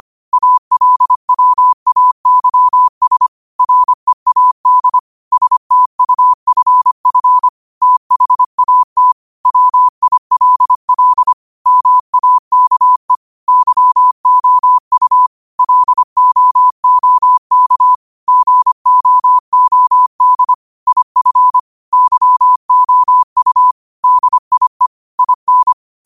Morse code, heard as a keyed tone, about 25 wpm, 1000 Hz. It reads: ALWAYS READ STUFF THAT WILL MAKE YOU LOOK GOOD IF YOU DIE IN
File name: qotdfiles / 25wpm